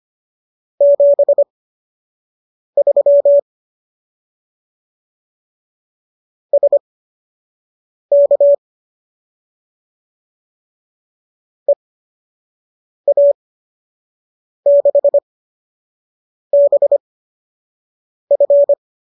Morse code, heard as a keyed tone, about 25 wpm, spaced as for 6 wpm, 570 Hz. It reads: 73 SK EA6BF